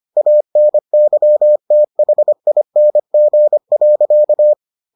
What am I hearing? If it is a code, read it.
Morse code, heard as a keyed tone, about 25 wpm, 600 Hz. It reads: ANYTHING.